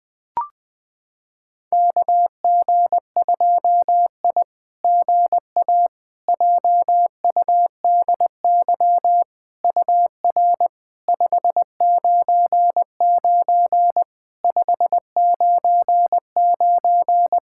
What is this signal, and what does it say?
Morse code, heard as a keyed tone, about 20 wpm, 700 Hz. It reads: KG2I GA JUDY UR 599 599
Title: SOTA QSO Practice 20wpm